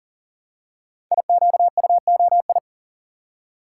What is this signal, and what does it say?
Morse code, heard as a keyed tone, about 41 wpm, 700 Hz. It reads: IQUOI
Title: Code Groups - 5 Letters - 40wpm